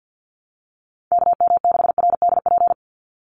Morse code, heard as a keyed tone, about 50 wpm, 700 Hz. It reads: KM6RDP
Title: USA Call Signs 50wpm